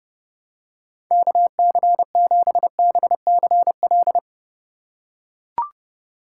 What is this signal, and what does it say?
Morse code, heard as a keyed tone, about 30 wpm, 700 Hz. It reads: KC7BCL
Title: USA Call Signs - 6 Characters - 4x Character Spacing - 30wpm